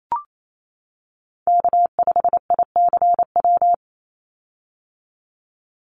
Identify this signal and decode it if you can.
Morse code, heard as a keyed tone, about 28 wpm, 700 Hz. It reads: K5ICW
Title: USA Call Signs 28wpm